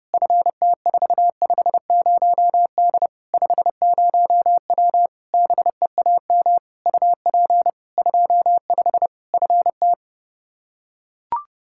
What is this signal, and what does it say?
Morse code, heard as a keyed tone, about 30 wpm, 700 Hz. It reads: FT450D 50W BEAM UP 25 FT
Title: Practice QSO - Exchange 2 - Info - 30wpm